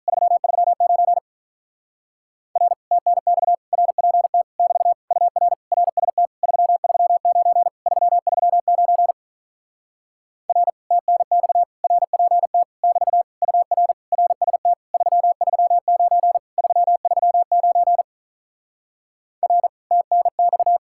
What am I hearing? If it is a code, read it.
Morse code, heard as a keyed tone, about 48 wpm, 700 Hz. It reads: 339 R TNX RPT <BT> UR RST 339 339 R TNX RPT <BT> UR RST 339 339 R TNX